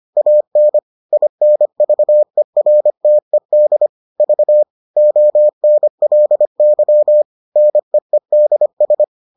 Morse code, heard as a keyed tone, about 25 wpm, 600 Hz. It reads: AN INVERTED V ONLY NEEDS